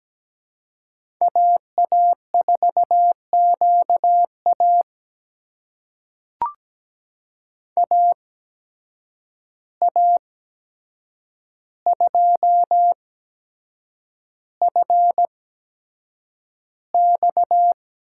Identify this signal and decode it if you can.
Morse code, heard as a keyed tone, about 17 wpm, spaced as for 7 wpm, 700 Hz. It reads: AA4QA AA2FX
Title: USA Call Signs - 5 Characters - 8x Character Spacing - 17wpm